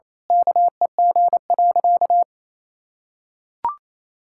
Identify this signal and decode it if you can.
Morse code, heard as a keyed tone, about 28 wpm, 700 Hz. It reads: KEG.
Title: Pangram Challenge v2 - 28wpm